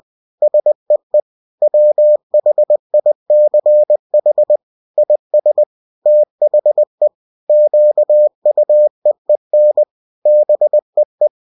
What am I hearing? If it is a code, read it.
Morse code, heard as a keyed tone, about 20 wpm, 600 Hz. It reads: SEE WHICH IS THE QUEEN BEE